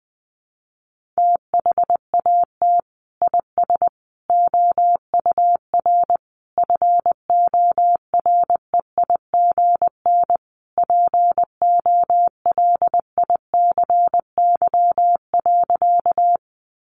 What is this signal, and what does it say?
Morse code, heard as a keyed tone, about 20 wpm, 700 Hz. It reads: THAT IS OUR FOREIGN POLICY.
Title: Sentences from Top 1400 Words 20wpm